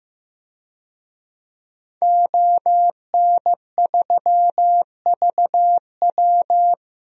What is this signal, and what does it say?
Morse code, heard as a keyed tone, about 15 wpm, 700 Hz. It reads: ON3VW
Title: DXCC-100 Call Signs - 5 Characters - 15wpm